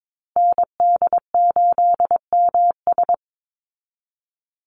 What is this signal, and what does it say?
Morse code, heard as a keyed tone, about 22 wpm, 700 Hz. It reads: ND8MS
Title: Call Signs Repeated 3x - 22wpm